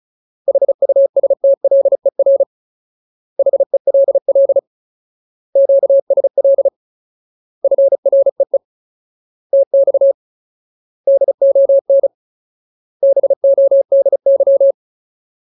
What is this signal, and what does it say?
Morse code, heard as a keyed tone, about 34 wpm, 550 Hz. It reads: HUSTLER HELL QSL FREE TX DON BODY
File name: Ultimate Word List / 35WPM Ultimate